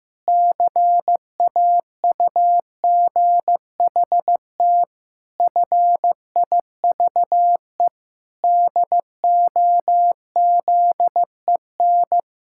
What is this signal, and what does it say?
Morse code, heard as a keyed tone, about 15 wpm, 700 Hz. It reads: CAUGHT FIVE DOZEN
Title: Pangram Challenge v2 - 15wpm